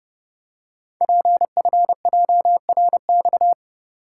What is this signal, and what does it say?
Morse code, heard as a keyed tone, about 30 wpm, 700 Hz. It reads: PFJRX
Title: Code Groups - 5 Letters - 30wpm